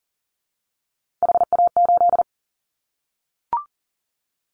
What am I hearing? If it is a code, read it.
Morse code, heard as a keyed tone, about 40 wpm, 700 Hz. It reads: HA8
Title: Callsign Prefixes for Top 25 DXCC - 40wpm